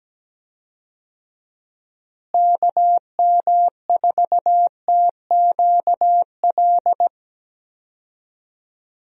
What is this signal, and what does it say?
Morse code, heard as a keyed tone, about 17 wpm, 700 Hz. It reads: KM4TQL
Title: USA Call Signs - 6 Characters - Repeated 3x - 17wpm